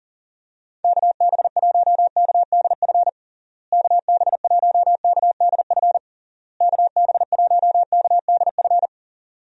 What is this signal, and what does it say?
Morse code, heard as a keyed tone, about 37 wpm, 700 Hz. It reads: KB1KDF KB1KDF KB1KDF
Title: USA Call Signs - 6 Characters - Repeated 3x - 40wpm